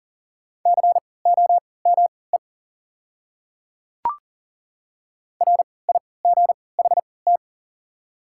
Morse code, heard as a keyed tone, about 39 wpm, spaced as for 22 wpm, 700 Hz. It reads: COME RIGHT
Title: 3x Character Spacing - Top 200 Words - 40wpm